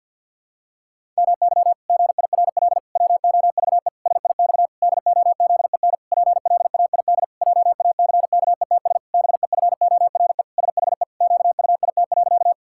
Morse code, heard as a keyed tone, about 50 wpm, 700 Hz. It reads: MY GIRL WOVE SIX DOZEN PLAID JACKETS BEFORE SHE QUIT.
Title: Pangram Challenge v2 - 50wpm